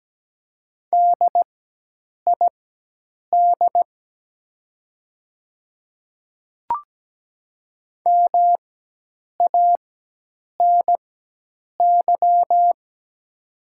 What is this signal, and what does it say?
Morse code, heard as a keyed tone, about 17 wpm, spaced as for 8 wpm, 700 Hz. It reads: DID MANY
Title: 4x Character Spacing - Top 200 Words - 17wpm